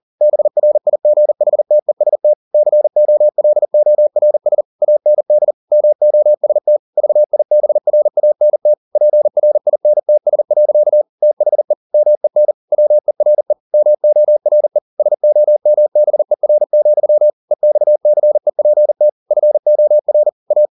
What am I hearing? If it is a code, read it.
Morse code, heard as a keyed tone, about 40 wpm, 600 Hz. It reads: BRIGHTEST COLORS AND MOST VIBRANT PRINTS. THE MEN WERE MORE SOMBER, EXCEPT FOR A